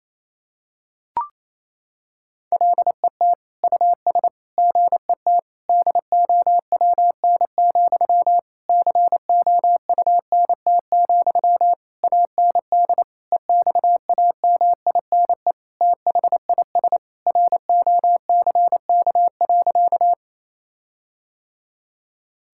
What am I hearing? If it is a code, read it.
Morse code, heard as a keyed tone, about 28 wpm, 700 Hz. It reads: LET US GET DOWN, COUNT, AND EXAMINE THIS ROCK.
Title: Sentences from Top 1600 Words 28wpm